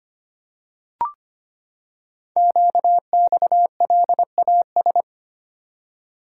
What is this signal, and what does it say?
Morse code, heard as a keyed tone, about 25 wpm, 700 Hz. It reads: QXLAS
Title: Code Groups - 5 Letters - 25wpm